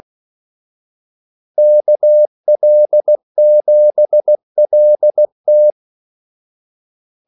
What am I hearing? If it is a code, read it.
Morse code, heard as a keyed tone, about 16 wpm, 600 Hz. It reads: KL7LT
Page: Kaldesignaler 16 wpm